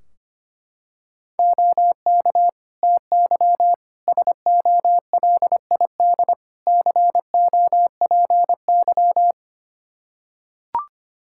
Morse code, heard as a keyed tone, about 25 wpm, 700 Hz. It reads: OK TY SOLID COPY
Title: K1USN SST QSO Practice 25wpm